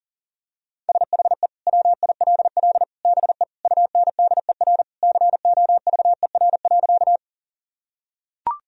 Morse code, heard as a keyed tone, about 40 wpm, 700 Hz. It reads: SHE WILL BE UNDER COVER.
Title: Sentences from Top 1000 Words - Review Part 1 - 40wpm